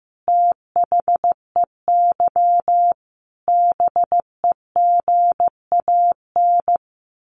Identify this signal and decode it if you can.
Morse code, heard as a keyed tone, about 15 wpm, 700 Hz. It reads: THEY BEGAN
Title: Sentences from Top 1400 Words 15wpm